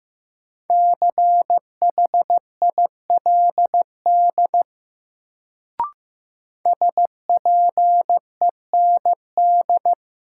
Morse code, heard as a keyed tone, about 15 wpm, 700 Hz. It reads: CHILD SPEND
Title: Sentences from Top 800 Words - New Word Drill - 15wpm